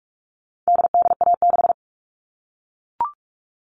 Morse code, heard as a keyed tone, about 44 wpm, 700 Hz. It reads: DDA6
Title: All Cut Numbers - 4-Digit 45wpm